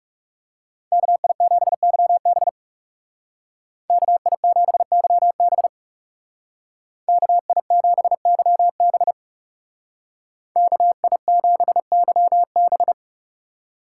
Morse code, heard as a keyed tone, about 39 wpm, 700 Hz. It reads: KI7YB KI7YB KI7YB KI7YB